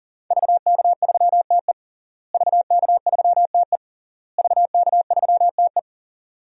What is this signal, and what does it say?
Morse code, heard as a keyed tone, about 40 wpm, 700 Hz. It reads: VK3TE VK3TE VK3TE
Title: DXCC-100 Call Signs - 5 Characters - Repeated 3x - 40wpm